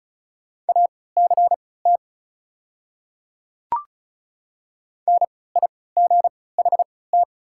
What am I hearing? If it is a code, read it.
Morse code, heard as a keyed tone, about 36 wpm, spaced as for 20 wpm, 690 Hz. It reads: ACT NIGHT
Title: 3x Character Spacing - Top 300 Words - 35wpm